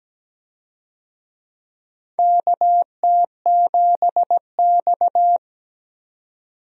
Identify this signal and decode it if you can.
Morse code, heard as a keyed tone, about 17 wpm, 700 Hz. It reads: KT7X
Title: USA Call Signs - 4 Characters - 17wpm